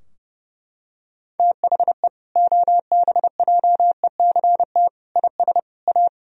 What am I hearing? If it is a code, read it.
Morse code, heard as a keyed tone, about 30 wpm, 700 Hz. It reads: THE OBJECT IS A
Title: Sentences from Top 1000 Words - Review Part 3 - 30wpm